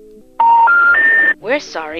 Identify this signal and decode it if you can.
Special information tone, with speech.